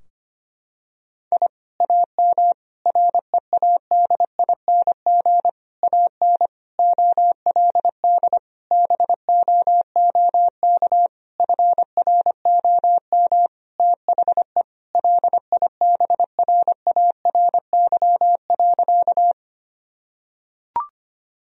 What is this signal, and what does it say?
Morse code, heard as a keyed tone, about 25 wpm, 700 Hz. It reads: I AM READING AN OLD BOOK FROM THE LIBRARY.
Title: Sentences from Top 1000 Words - Review Part 4 - 25wpm